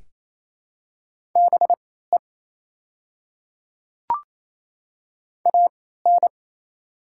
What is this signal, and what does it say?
Morse code, heard as a keyed tone, about 27 wpm, spaced as for 16 wpm, 690 Hz. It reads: BE AN
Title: 3x Character Spacing - Top 100 Words - 28wpm